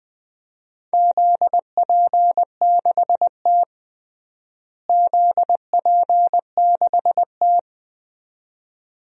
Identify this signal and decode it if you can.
Morse code, heard as a keyed tone, about 20 wpm, 700 Hz. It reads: ZP6T ZP6T
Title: DXCC-100 Call Signs - 4 Characters - Repeated 2x - 20wpm